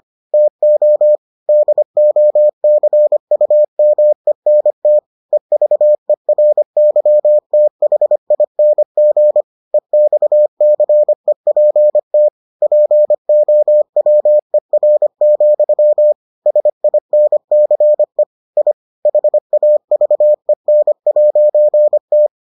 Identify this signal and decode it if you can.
Morse code, heard as a keyed tone, about 25 wpm, 600 Hz. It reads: TO DOCUMENT EVERYTHING EXCEPT POWER, SINCE I HAVEN'T